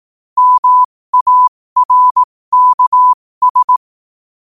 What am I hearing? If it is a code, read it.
Morse code, heard as a keyed tone, about 18 wpm, spaced as for 15 wpm, 1000 Hz. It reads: MARKS
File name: qotdfiles / 15wpm